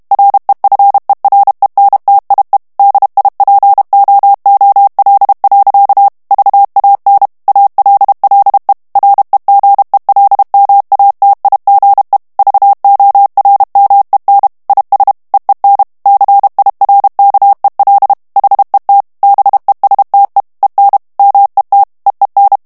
Morse code, heard as a keyed tone, about 32 wpm, 800 Hz. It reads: REFERENTIE DIPOOL. VAN ALLE REGELMATIGE VORMEN IS EEN CIRKEL HET BESTE EN MET EEN
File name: HOLLANDAIS texte clair 1 vitesse 30 mots minute